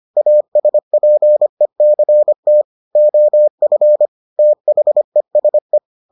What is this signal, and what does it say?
Morse code, heard as a keyed tone, about 25 wpm, 600 Hz. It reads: ASPECT OF THESE